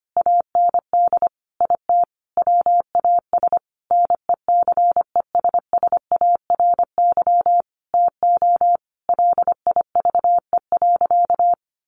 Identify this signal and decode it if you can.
Morse code, heard as a keyed tone, about 25 wpm, 700 Hz. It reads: AND IT WAS NECESSARY TO LIVE.
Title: Sentences from Top 800 Words 25wpm